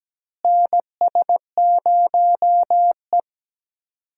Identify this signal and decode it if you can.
Morse code, heard as a keyed tone, about 17 wpm, 700 Hz. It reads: NS0E
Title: USA Call Signs - 4 Characters - Repeated 2x - 17wpm